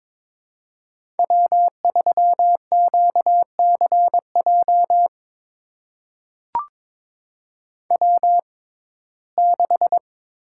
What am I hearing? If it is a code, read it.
Morse code, heard as a keyed tone, about 22 wpm, spaced as for 16 wpm, 700 Hz. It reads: W3QCJ W6
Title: USA Call Signs - 5 Characters - 6x Character Spacing - 22wpm